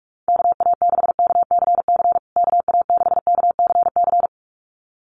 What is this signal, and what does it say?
Morse code, heard as a keyed tone, about 44 wpm, 700 Hz. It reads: KA6KCC KA6KCC
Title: USA Call Signs Repeated 2x 45wpm